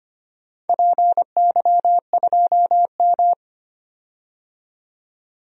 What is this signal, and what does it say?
Morse code, heard as a keyed tone, about 25 wpm, 700 Hz. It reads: PY2M
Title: DXCC-100 Call Signs - 4 Characters - 25wpm